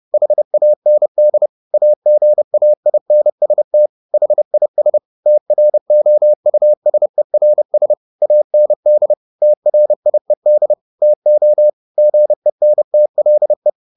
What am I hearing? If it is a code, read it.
Morse code, heard as a keyed tone, about 30 wpm, 600 Hz. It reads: HAND AGAINST HIS TROUSERS AND TRIED TO GENTLE
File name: the old man and the sea / the old man and the sea 1 30wpm